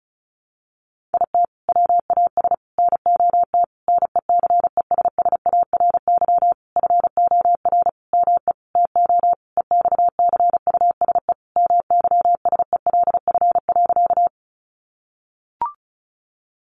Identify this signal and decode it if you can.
Morse code, heard as a keyed tone, about 34 wpm, 700 Hz. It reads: IT WAS NOT NECESSARY FOR ME TO EXCUSE MYSELF.
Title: Sentences from Top 1300 Words 35wpm